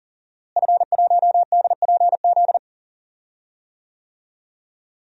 Morse code, heard as a keyed tone, about 39 wpm, 700 Hz. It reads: F1DPZ
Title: DXCC-100 Call Signs - 5 Characters - Repeated 3x - 40wpm